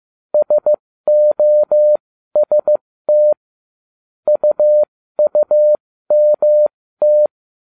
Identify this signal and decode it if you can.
Morse code, heard as a keyed tone, about 15 wpm, spaced as for 13 wpm, 600 Hz. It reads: SOST UUMT